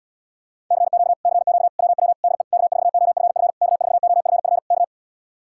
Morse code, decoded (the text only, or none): CQ CQ WW DE K4OFF K4OFF K